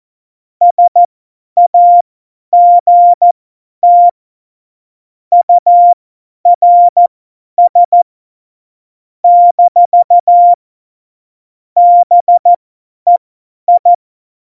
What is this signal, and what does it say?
Morse code, heard as a keyed tone, about 14 wpm, spaced as for 10 wpm, 700 Hz. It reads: SAGT URS - BEI
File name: Morsekurs / Witze / Witz 21. Mai 2019, 50 F70